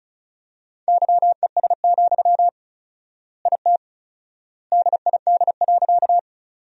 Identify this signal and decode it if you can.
Morse code, heard as a keyed tone, about 36 wpm, 700 Hz. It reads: YES, IT DID.
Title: 4x Word-Spacing - Sentences from Top 500 Words - 35wpm